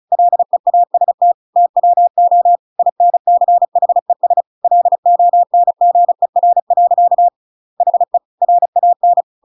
Morse code, heard as a keyed tone, about 34 wpm, 700 Hz. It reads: LEAST TWO INCHES LONGER. HE RAN
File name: Pinocchio / Pinocchio ch5 35 35 ik1hge